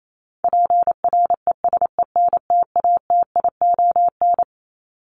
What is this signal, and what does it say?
Morse code, heard as a keyed tone, about 28 wpm, 700 Hz. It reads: PRESENTATION
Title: Top 1501-1750 Words 28wpm